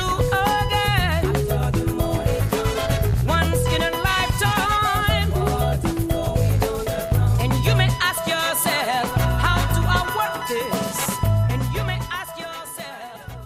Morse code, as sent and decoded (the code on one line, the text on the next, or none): none